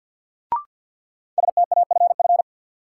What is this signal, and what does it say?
Morse code, heard as a keyed tone, about 53 wpm, 700 Hz. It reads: STAFF
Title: Sentences from Top 1200 Words - New Word Drill Alt - 50wpm